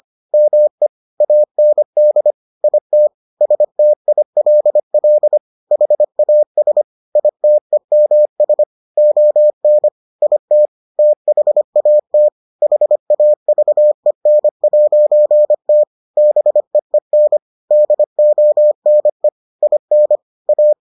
Morse code, heard as a keyed tone, about 25 wpm, 600 Hz. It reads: ME AND IT STILL HAS ITEMS ON IT THAT HAVEN'T BEEN DONE IN A